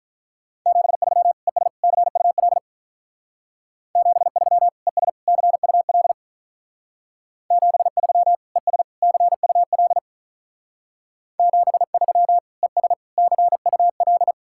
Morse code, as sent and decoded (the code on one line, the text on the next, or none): --... ...-- / . ... / -.-. ..- .-.. / --... ...-- / . ... / -.-. ..- .-.. / --... ...-- / . ... / -.-. ..- .-.. / --... ...-- / . ... / -.-. ..- .-..
73 ES CUL 73 ES CUL 73 ES CUL 73 ES CUL